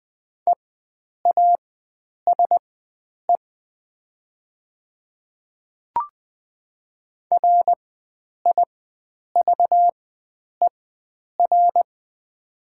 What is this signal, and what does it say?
Morse code, heard as a keyed tone, about 20 wpm, spaced as for 9 wpm, 700 Hz. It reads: EASE RIVER
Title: 4x Character Spacing - Top 300 Words - 20wpm